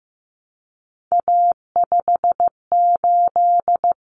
Morse code, heard as a keyed tone, about 15 wpm, 700 Hz. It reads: A58
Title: Common Cut Numbers - 3-Digit 15wpm